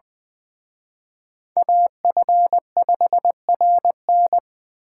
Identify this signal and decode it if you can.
Morse code, heard as a keyed tone, about 24 wpm, 700 Hz. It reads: AF5RN
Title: Speed Racing - USA 2x2 Call Signs 20wpm